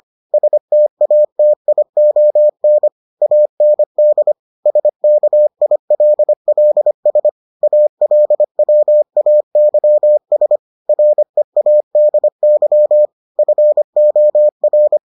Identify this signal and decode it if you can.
Morse code, heard as a keyed tone, about 25 wpm, 600 Hz. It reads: STATION AND SKILLS ALWAYS READY FOR